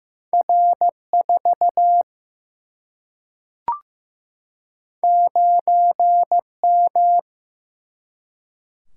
Morse code, heard as a keyed tone, about 15 wpm, 700 Hz. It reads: R 4 9 M
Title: Single-Letter Single-Number 15wpm